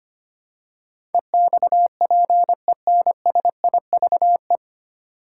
Morse code, heard as a keyed tone, about 25 wpm, 700 Hz. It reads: EXPENSIVE
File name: Nine Letter Words - 25wpm